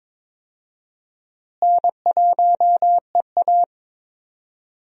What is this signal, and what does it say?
Morse code, heard as a keyed tone, about 22 wpm, 700 Hz. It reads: N1EA
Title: USA Call Signs - 4 Characters - Repeated 2x - 22wpm